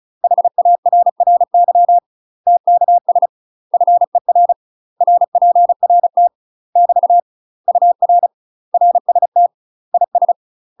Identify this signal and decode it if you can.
Morse code, heard as a keyed tone, about 36 wpm, 700 Hz. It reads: HARRY TKS FER RPRT <BT> UR RST IS
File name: UK MASTER QSOs TRAINING / 35WPM / UK MASTER QSOs 0009